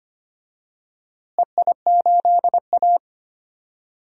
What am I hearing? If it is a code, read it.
Morse code, heard as a keyed tone, about 25 wpm, 700 Hz. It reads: EI8A
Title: DXCC-100 Call Signs - 4 Characters - Repeated 2x - 25wpm